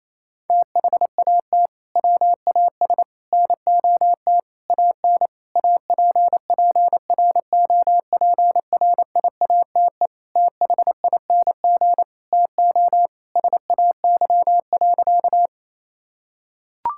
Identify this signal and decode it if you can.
Morse code, heard as a keyed tone, about 28 wpm, 700 Hz. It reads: THAT WAS NOT AN APPROPRIATE THING TO SAY.
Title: Sentences from Top 1000 Words - Review Part 5 - 28wpm